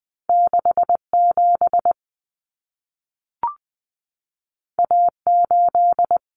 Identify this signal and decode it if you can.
Morse code, heard as a keyed tone, about 20 wpm, 700 Hz. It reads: 67 A8
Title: Common Cut Numbers - 2-Digit 20wpm